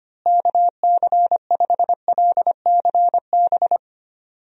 Morse code, heard as a keyed tone, about 25 wpm, 700 Hz. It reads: KC5LCB
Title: USA Call Signs - 6 Characters - Repeated 2x - 25wpm